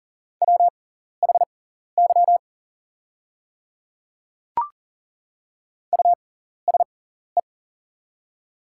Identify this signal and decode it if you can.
Morse code, heard as a keyed tone, about 40 wpm, spaced as for 14 wpm, 690 Hz. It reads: WHY USE
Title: 6x Character Spacing - Top 300 Words - 40wpm